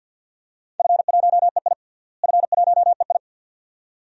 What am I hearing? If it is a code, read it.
Morse code, heard as a keyed tone, about 50 wpm, 700 Hz. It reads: F1EI F1EI
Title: DXCC-100 Call Signs - 4 Characters - Repeated 2x - 50wpm